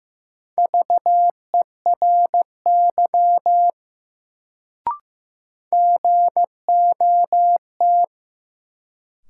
Morse code, heard as a keyed tone, about 15 wpm, 700 Hz. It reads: VERY GOT
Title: Sets of 2 Words - New Word Drill - 15wpm